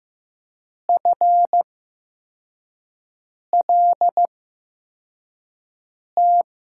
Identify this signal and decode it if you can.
Morse code, heard as a keyed tone, about 15 wpm, spaced as for 4 wpm, 700 Hz. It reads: FLT
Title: USA Call Signs - 5 Characters - 8x Character Spacing - 15wpm